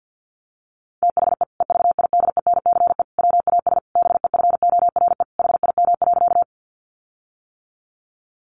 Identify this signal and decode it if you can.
Morse code, heard as a keyed tone, about 50 wpm, 700 Hz. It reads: THE EVIDENCE WAS BEFORE HIM.
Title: Sentences from Top 1300 Words 50wpm